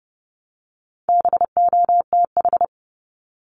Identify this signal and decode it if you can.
Morse code, heard as a keyed tone, about 30 wpm, 700 Hz. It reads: BOTH
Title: Top 500 Words Repeated 30wpm